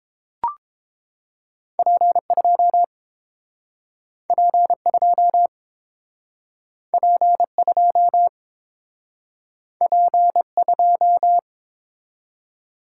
Morse code, heard as a keyed tone, about 31 wpm, 700 Hz. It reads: P2 P2 P2 P2